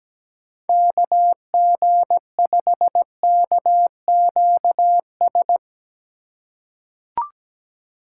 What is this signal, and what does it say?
Morse code, heard as a keyed tone, about 17 wpm, 700 Hz. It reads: KG5KQS
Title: USA Call Signs - 6 Characters - 2x Character Spacing - 17wpm